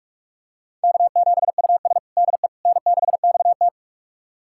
Morse code, heard as a keyed tone, about 47 wpm, 700 Hz. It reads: K7US DE N6XT
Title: USA Callsign DE Callsign - 4 Characters - 45wpm